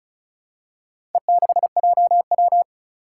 Morse code, heard as a keyed tone, about 36 wpm, 700 Hz. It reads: E6JW
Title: Four Challenging Letter-Number Combinations 35wpm